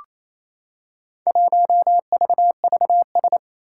Morse code, heard as a keyed tone, about 27 wpm, 700 Hz. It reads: 1VVS
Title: Four Challenging Letter-Number Combinations 28wpm